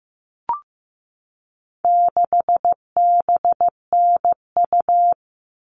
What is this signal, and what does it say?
Morse code, heard as a keyed tone, about 15 wpm, 700 Hz. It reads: 6BNU
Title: All Cut Numbers - 4-Digit 15wpm